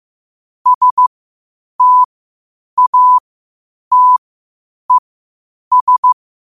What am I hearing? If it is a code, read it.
Morse code, heard as a keyed tone, about 15 wpm, spaced as for 8 wpm, 1000 Hz. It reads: STATES
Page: New quotes every day in morse code at 8 Words per minute.